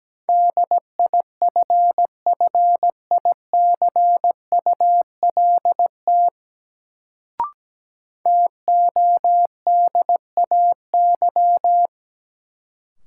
Morse code, heard as a keyed tone, about 17 wpm, 700 Hz. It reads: DIFFICULT TODAY
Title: Sentences from Top 600 Words - New Word Drill - 17wpm